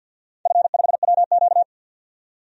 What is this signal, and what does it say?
Morse code, heard as a keyed tone, about 52 wpm, 700 Hz. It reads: F5WQ
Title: DXCC-100 Call Signs - 4 Characters - 50wpm